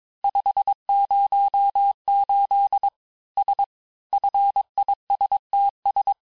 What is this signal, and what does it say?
Morse code, heard as a keyed tone, about 22 wpm, 800 Hz. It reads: 5 0 8 S FISTS
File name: cours-du-mercredi / audios / cours-nr-860-FIN-de-QSOr-vitesse-21MPM